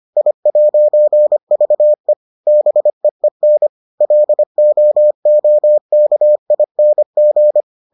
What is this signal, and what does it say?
Morse code, heard as a keyed tone, about 25 wpm, 600 Hz. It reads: I'VE BEEN LOOKING